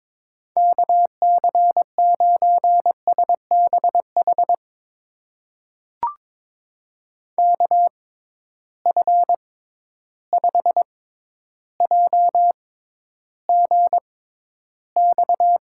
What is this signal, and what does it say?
Morse code, heard as a keyed tone, about 22 wpm, spaced as for 11 wpm, 700 Hz. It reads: KC9SBH KF5JGX
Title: USA Call Signs - 6 Characters - 6x Character Spacing - 22wpm